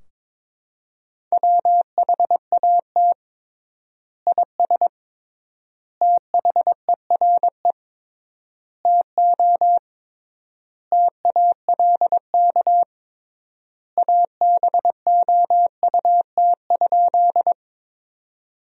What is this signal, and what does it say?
Morse code, heard as a keyed tone, about 22 wpm, 700 Hz. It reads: WHAT IS THERE TO TALK ABOUT?
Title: 3x Word-Spacing - Sentences from Top 400 Words - 22wpm